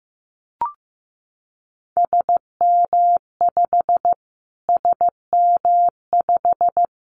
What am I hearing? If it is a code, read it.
Morse code, heard as a keyed tone, about 15 wpm, 700 Hz. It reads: SM5 SM5
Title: Callsign Prefixes for Top 25 DXCC - Repeated 3x - 15wpm